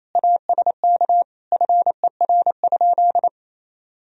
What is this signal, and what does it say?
Morse code, heard as a keyed tone, about 28 wpm, 700 Hz. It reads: ASK FER?
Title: Sentences from Top 200 Words and Abbreviations 28wpm